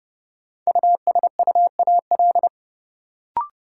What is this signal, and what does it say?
Morse code, heard as a keyed tone, about 32 wpm, 700 Hz. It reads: USUAL
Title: Sentences from Top 400 Words - New Word Drill - 30wpm